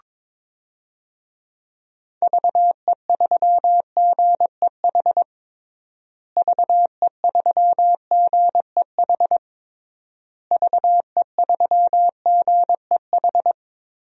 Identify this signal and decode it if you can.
Morse code, heard as a keyed tone, about 22 wpm, 700 Hz. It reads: VE3GEH VE3GEH VE3GEH
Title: DXCC-100 Call Signs - 6 Characters - Repeated 3x - 22wpm